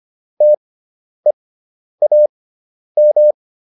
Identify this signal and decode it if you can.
Morse code, heard as a keyed tone, about 25 wpm, spaced as for 10 wpm, 600 Hz. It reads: TEAM